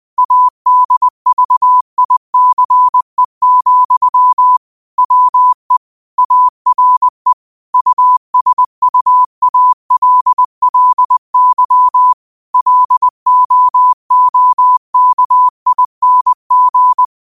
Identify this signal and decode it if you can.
Morse code, heard as a keyed tone, about 20 wpm, 1000 Hz. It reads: ADVICE, WE ARE USUALLY LOOKING